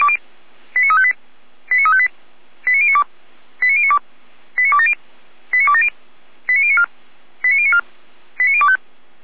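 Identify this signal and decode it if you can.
Five-tone selective call (ZVEI), heard as quick runs of five tones, tones of about 70 ms. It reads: 89358 89358 89092 89092 89270 89270 89094 89094 89025
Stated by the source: ZVEI five-tone selective call received by SDR, from Feuerwehr Orlamünde.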